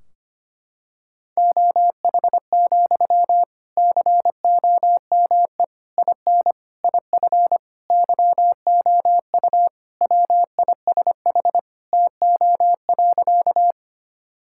Morse code, heard as a keyed tone, about 25 wpm, 700 Hz. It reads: OH, COME IN IF YOU WISH TO.
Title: Sentences from Top 1600 Words 25wpm